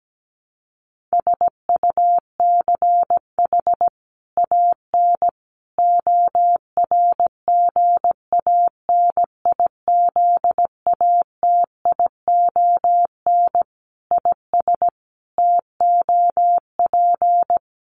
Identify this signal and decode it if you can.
Morse code, heard as a keyed tone, about 17 wpm, 700 Hz. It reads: SUCH AN ORGANIZATION IS TOP
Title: Sentences from Top 1100 Words 17wpm